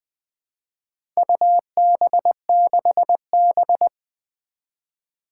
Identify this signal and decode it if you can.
Morse code, heard as a keyed tone, about 20 wpm, 700 Hz. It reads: UB6B
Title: Four Challenging Letter-Number Combinations 20wpm